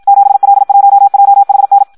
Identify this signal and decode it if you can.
Morse code, heard as a keyed tone, about 56 wpm, 800 Hz. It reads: 7L1JHN
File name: Audio practica CW / qrq-tx-54 2